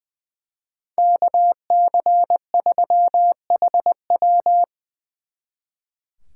Morse code, heard as a keyed tone, about 24 wpm, 700 Hz. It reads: KC3HW
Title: Speed Racing - USA 2x2 Call Signs 20wpm